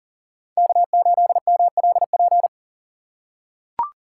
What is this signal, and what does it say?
Morse code, heard as a keyed tone, about 40 wpm, 700 Hz. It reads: K8MLP
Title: USA Call Signs - 5 Characters - 2x Character Spacing - 40wpm